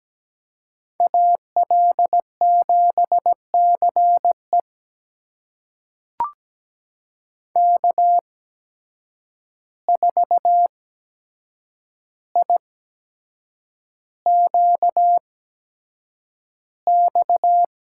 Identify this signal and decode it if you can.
Morse code, heard as a keyed tone, about 17 wpm, spaced as for 7 wpm, 700 Hz. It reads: AL7CE K4IQX
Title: USA Call Signs - 5 Characters - 8x Character Spacing - 17wpm